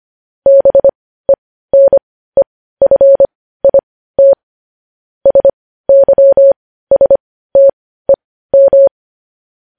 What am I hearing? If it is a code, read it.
Morse code, heard as a keyed tone, about 25 wpm, spaced as for 15 wpm, 550 Hz. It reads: BENEFIT SYSTEM
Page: news headlines in morse code at 15 wpm